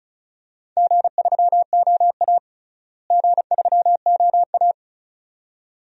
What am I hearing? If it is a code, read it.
Morse code, heard as a keyed tone, about 36 wpm, 700 Hz. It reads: G3OA G3OA
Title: DXCC-100 Call Signs - 4 Characters - Repeated 2x - 35wpm